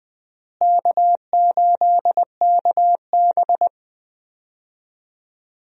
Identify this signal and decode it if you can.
Morse code, heard as a keyed tone, about 20 wpm, 700 Hz. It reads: K8KB
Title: USA Call Signs - 4 Characters - Repeated 2x - 20wpm